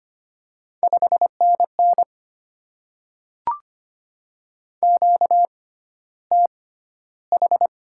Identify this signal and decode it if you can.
Morse code, heard as a keyed tone, about 25 wpm, spaced as for 13 wpm, 700 Hz. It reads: 5NN QTH
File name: Top 20 QSO Elements - 6x Character Spacing - 25wpm